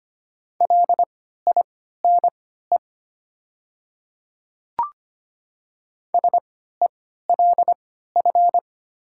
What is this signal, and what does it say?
Morse code, heard as a keyed tone, about 25 wpm, spaced as for 14 wpm, 700 Hz. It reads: LINE SELF
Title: 3x Character Spacing - Top 200 Words - 25wpm